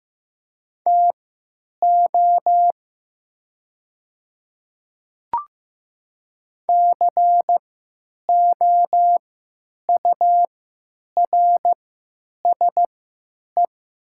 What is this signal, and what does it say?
Morse code, heard as a keyed tone, about 15 wpm, spaced as for 9 wpm, 700 Hz. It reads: TO COURSE
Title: 3x Character Spacing - Top 500 Words - 15wpm